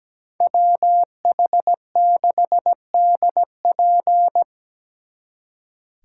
Morse code, heard as a keyed tone, about 20 wpm, 700 Hz. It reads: WH6DP
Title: Speed Racing - USA 2x2 Call Signs 17wpm